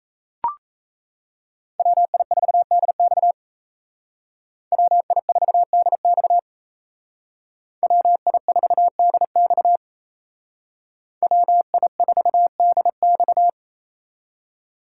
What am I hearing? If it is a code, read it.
Morse code, heard as a keyed tone, about 36 wpm, 700 Hz. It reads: WI4DX WI4DX WI4DX WI4DX